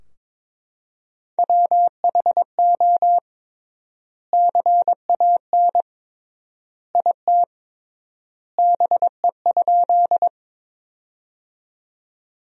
Morse code, heard as a keyed tone, about 21 wpm, 700 Hz. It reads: WHO CAN IT BE?
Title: 3x Word-Spacing - Sentences from Top 100 Words - 22wpm